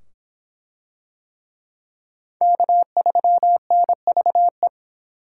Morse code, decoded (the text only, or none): K3NVE